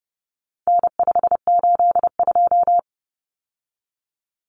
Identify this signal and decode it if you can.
Morse code, heard as a keyed tone, about 30 wpm, 700 Hz. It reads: N582
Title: Common Cut Numbers - 4-Digit 30wpm